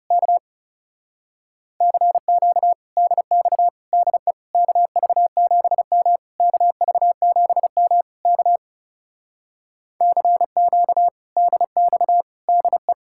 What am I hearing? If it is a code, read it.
Morse code, heard as a keyed tone, about 34 wpm, 700 Hz. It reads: K CQ DX DE KV7M KV7M K CQ DX DE
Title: Speed-Racing Common QSO Phrases - 30wpm